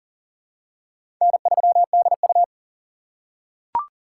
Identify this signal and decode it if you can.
Morse code, heard as a keyed tone, about 40 wpm, 700 Hz. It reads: N3DU
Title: USA Call Signs - 6x Character Spacing - 40wpm